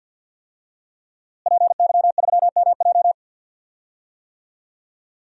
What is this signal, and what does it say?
Morse code, heard as a keyed tone, about 47 wpm, 700 Hz. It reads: PY3MJ
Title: DXCC-100 Call Signs - 5 Characters - Repeated 2x - 50wpm